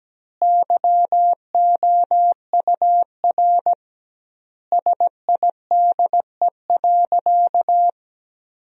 Morse code, heard as a keyed tone, about 17 wpm, 700 Hz. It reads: YOUR SIDE.
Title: 2x Word-Spacing - Sentences from Top 400 Words - 17wpm